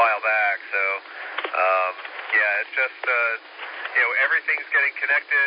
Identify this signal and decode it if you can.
Emergency Alert System (SAME) tones (header bursts), heard as harsh buzzing bursts, with speech. It is heard from 0.3 to 4.6 s.